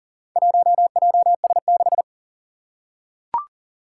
Morse code, heard as a keyed tone, about 40 wpm, 700 Hz. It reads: 1JS6
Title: Four Challenging Letter-Number Combinations 40wpm